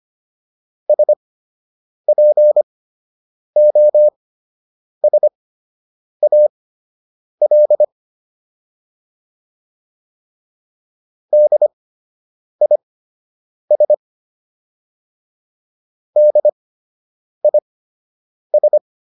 Morse code, heard as a keyed tone, about 25 wpm, spaced as for 8 wpm, 600 Hz. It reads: SPOSAL DIS DIS